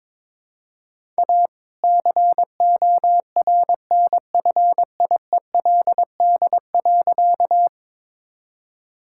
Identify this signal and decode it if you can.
Morse code, heard as a keyed tone, about 22 wpm, 700 Hz. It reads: A CORNFIELD.
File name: All-Age Appropriate Jokes - 22wpm